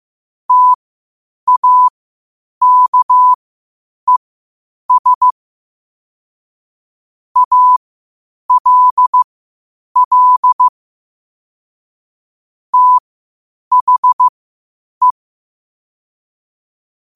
Morse code, heard as a keyed tone, about 15 wpm, spaced as for 8 wpm, 1000 Hz. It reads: TAKES ALL THE
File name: qotdfiles / 8wpm